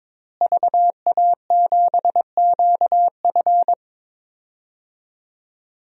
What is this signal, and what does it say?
Morse code, heard as a keyed tone, about 22 wpm, 700 Hz. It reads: VA7QF
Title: DXCC-100 Call Signs - 5 Characters - Repeated 3x - 22wpm